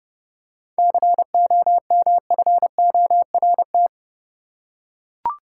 Morse code, heard as a keyed tone, about 30 wpm, 700 Hz. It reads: COMFORT
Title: Seven Letter Words - 30wpm